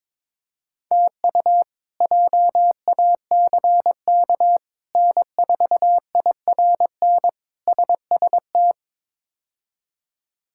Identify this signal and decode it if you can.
Morse code, heard as a keyed tone, about 22 wpm, 700 Hz. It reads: TU JACK N4IRN SST
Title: K1USN SST QSO Practice - 22wpm